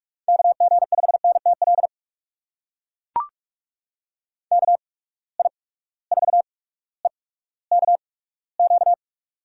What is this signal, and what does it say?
Morse code, heard as a keyed tone, about 42 wpm, spaced as for 19 wpm, 700 Hz. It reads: KG5NTL KI4EKQ
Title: USA Call Signs - 6 Characters - 8x Character Spacing - 45wpm